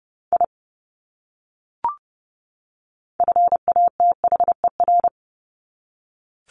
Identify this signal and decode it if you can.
Morse code, heard as a keyed tone, about 30 wpm, 700 Hz. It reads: I FATHER
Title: Top 200 Words 30wpm